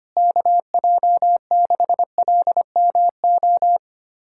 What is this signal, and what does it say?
Morse code, heard as a keyed tone, about 25 wpm, 700 Hz. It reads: KJ6LMO
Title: USA Callsign DE Callsign - 6 Characters - 25wpm